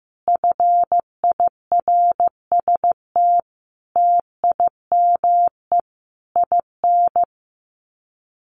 Morse code, heard as a keyed tone, about 15 wpm, 700 Hz. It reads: FIRST TIME IN
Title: Sets of 5 Words 15wpm